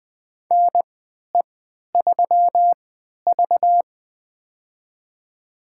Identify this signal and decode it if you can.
Morse code, heard as a keyed tone, about 20 wpm, spaced as for 11 wpm, 700 Hz. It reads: NE3V
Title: USA Call Signs - 3x Character Spacing - 20wpm